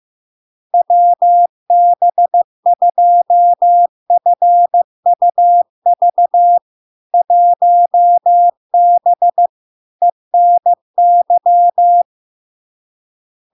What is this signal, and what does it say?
Morse code, heard as a keyed tone, about 15 wpm, 700 Hz. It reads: WB2FUV 1B ENY
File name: LARK-FD-2022-Log-in-CW-15wpm-700-Hz